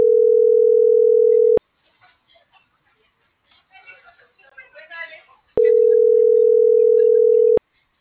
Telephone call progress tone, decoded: ringback tone

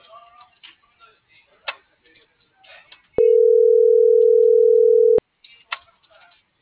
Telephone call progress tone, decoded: ringback tone

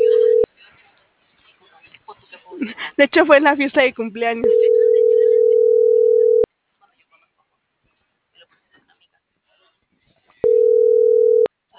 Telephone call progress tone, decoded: ringback tone

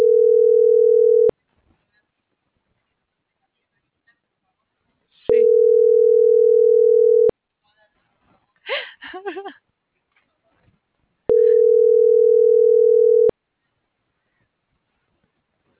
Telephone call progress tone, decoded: ringback tone

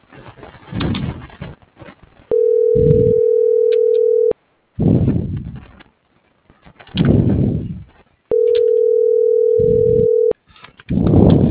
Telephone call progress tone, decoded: ringback tone

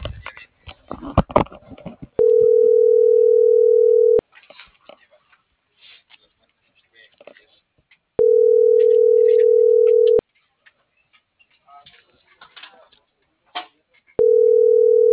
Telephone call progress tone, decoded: ringback tone

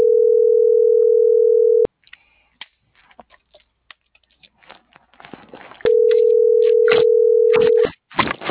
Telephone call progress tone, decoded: ringback tone

